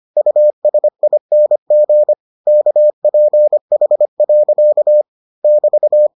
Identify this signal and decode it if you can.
Morse code, heard as a keyed tone, about 25 wpm, 600 Hz. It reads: USING KPH. <BT>